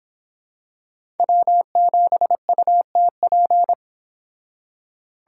Morse code, heard as a keyed tone, about 26 wpm, 700 Hz. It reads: W7UTP